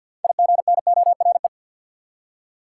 Morse code, heard as a keyed tone, about 51 wpm, 700 Hz. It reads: IGNORE
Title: Six Letter Words - 50wpm